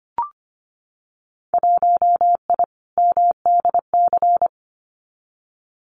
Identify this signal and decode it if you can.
Morse code, heard as a keyed tone, about 25 wpm, 700 Hz. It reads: 1I MDC
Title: Winter Field Day Exchanges 25wpm